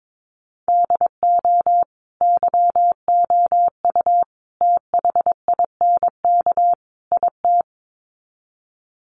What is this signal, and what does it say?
Morse code, heard as a keyed tone, about 22 wpm, 700 Hz. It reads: DO YOU THINK IT
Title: Sets of 4 Words 22wpm